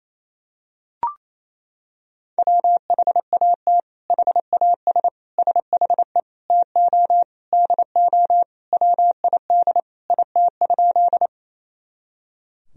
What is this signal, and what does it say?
Morse code, heard as a keyed tone, about 28 wpm, 700 Hz. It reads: WHAT HAS SHE TO DO WID IT?
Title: Sentences from Top 100 Words and Abbreviations 28wpm